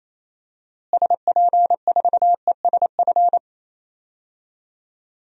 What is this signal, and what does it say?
Morse code, heard as a keyed tone, about 28 wpm, 700 Hz. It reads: SP4ESF
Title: DXCC-100 Call Signs - 6 Characters - Repeated 2x - 28wpm